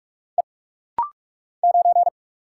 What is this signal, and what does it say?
Morse code, heard as a keyed tone, about 44 wpm, 700 Hz. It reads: E 9
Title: Single Letter-Number Alt - Rapid-Fire 45wpm